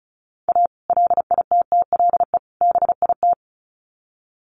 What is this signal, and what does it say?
Morse code, heard as a keyed tone, about 34 wpm, 700 Hz. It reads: A LITTLE BIT
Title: Sets of 3 Words 35wpm